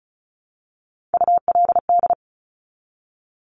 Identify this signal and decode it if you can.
Morse code, heard as a keyed tone, about 36 wpm, 690 Hz. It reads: ULD
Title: Top Three Letter Combinations in Words 35wpm